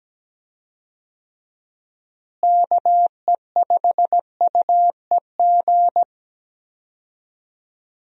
Morse code, heard as a keyed tone, about 17 wpm, 700 Hz. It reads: KE5UEG
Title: USA Call Signs - 6 Characters - Repeated 2x - 17wpm